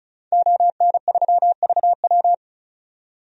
Morse code, heard as a keyed tone, about 34 wpm, 700 Hz. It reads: ON3VW
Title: DXCC-100 Call Signs - 5 Characters - 35wpm